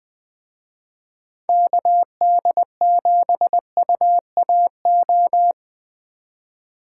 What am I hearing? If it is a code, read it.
Morse code, heard as a keyed tone, about 19 wpm, 700 Hz. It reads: KD7UAO
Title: Speed Racing - USA 2x3 Call Signs 17wpm